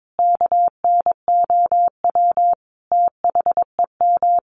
Morse code, heard as a keyed tone, about 22 wpm, 700 Hz. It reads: KNOW THEM
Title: Sentences from Top 1400 Words 22wpm